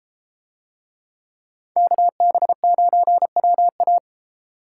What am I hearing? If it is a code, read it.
Morse code, heard as a keyed tone, about 33 wpm, 700 Hz. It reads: KB9WA